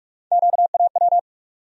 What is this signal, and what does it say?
Morse code, heard as a keyed tone, about 42 wpm, 700 Hz. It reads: QAW
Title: USA Call Sign Postfixes - 3 Letters 45wpm